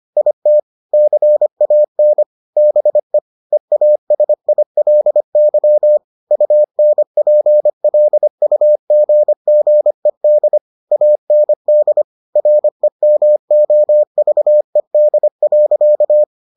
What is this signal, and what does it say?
Morse code, heard as a keyed tone, about 25 wpm, 600 Hz. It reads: IT CAN BE EASILY UNPLUGGED AND REMOVED.